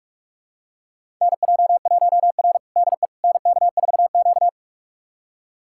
Morse code, heard as a keyed tone, about 43 wpm, 700 Hz. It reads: NJ1R DE NK4Q
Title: USA Callsign DE Callsign - 4 Characters - 45wpm